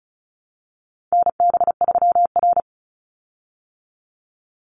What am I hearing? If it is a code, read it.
Morse code, heard as a keyed tone, about 34 wpm, 700 Hz. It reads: NB3R
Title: Call Signs Repeated 2x - 35wpm